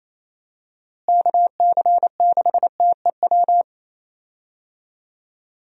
Morse code, heard as a keyed tone, about 28 wpm, 700 Hz. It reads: KC6TEW
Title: USA Call Signs - 6 Characters - Repeated 3x - 28wpm